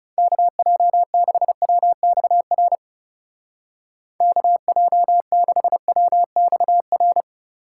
Morse code, heard as a keyed tone, about 35 wpm, 700 Hz. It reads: KJ6WXR KJ6WXR
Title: Speed Racing - USA 2x3 Call Signs 30wpm